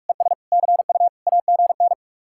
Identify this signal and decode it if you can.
Morse code, heard as a keyed tone, about 49 wpm, 700 Hz. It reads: ES CU AGN